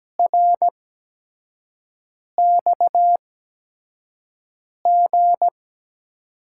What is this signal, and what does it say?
Morse code, heard as a keyed tone, about 17 wpm, spaced as for 5 wpm, 700 Hz. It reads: RXG
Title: USA Call Signs - 5 Characters - 8x Character Spacing - 17wpm